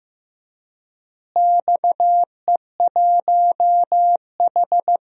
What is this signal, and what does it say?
Morse code, heard as a keyed tone, about 15 wpm, 700 Hz. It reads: XE1H
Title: DXCC-100 Call Signs - 4 Characters - Repeated 3x - 15wpm